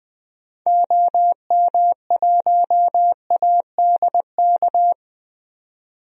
Morse code, heard as a keyed tone, about 20 wpm, 700 Hz. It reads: OM1ADK
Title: DXCC-100 Call Signs - 6 Characters - 20wpm